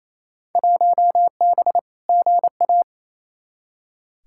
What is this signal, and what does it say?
Morse code, heard as a keyed tone, about 28 wpm, 700 Hz. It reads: 1B GA
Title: Field Day Exchanges Encore - 28wpm